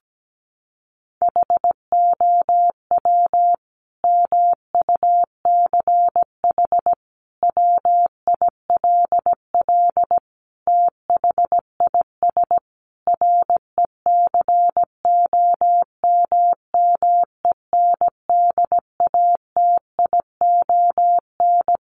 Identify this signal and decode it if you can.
Morse code, heard as a keyed tone, about 17 wpm, 700 Hz. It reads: HOW MUCH WILL THIS RECOMMENDATION
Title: Sentences from Top 1400 Words 17wpm